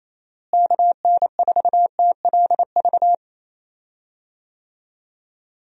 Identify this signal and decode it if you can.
Morse code, heard as a keyed tone, about 27 wpm, 700 Hz. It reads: KN4TLV
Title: USA Call Signs - 6 Characters - Repeated 2x - 28wpm